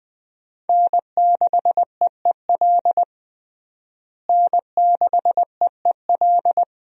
Morse code, heard as a keyed tone, about 21 wpm, 700 Hz. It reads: N6EEL N6EEL
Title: USA Call Signs - 5 Characters - Repeated 2x - 20wpm